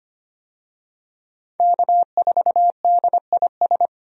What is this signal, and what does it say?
Morse code, heard as a keyed tone, about 25 wpm, 700 Hz. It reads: K4DIS
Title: USA Call Signs - 5 Characters - 25wpm